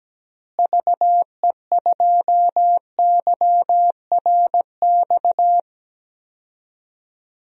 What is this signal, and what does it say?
Morse code, heard as a keyed tone, about 17 wpm, 700 Hz. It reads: VE2YRX
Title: DXCC-100 Call Signs - 6 Characters - Repeated 3x - 17wpm